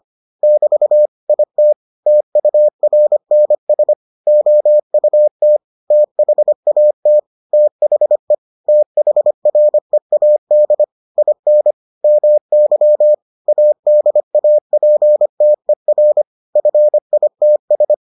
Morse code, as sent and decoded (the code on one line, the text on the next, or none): -...- / .. - / - ..- .-. -. ... / --- ..- - / - .... .- - / - .... / - .... .-. .- -.. / .. -. / -- -.-- / .- -.. .- .--. - . .-. / ..-. .. - ...
<BT> IT TURNS OUT THAT THE THREAD IN MY ADAPTER FITS